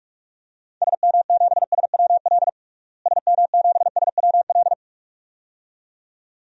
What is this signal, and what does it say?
Morse code, heard as a keyed tone, about 46 wpm, 690 Hz. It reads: SM7SWL SM7SWL
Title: DXCC-100 Call Signs - 6 Characters - Repeated 2x - 45wpm